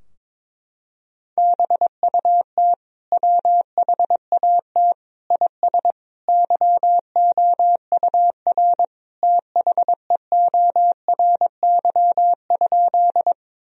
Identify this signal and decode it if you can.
Morse code, heard as a keyed tone, about 22 wpm, 700 Hz. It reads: BUT WHAT IS YOUR THEORY?
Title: Sentences from Top 1000 Words - Review Part 3 - 22wpm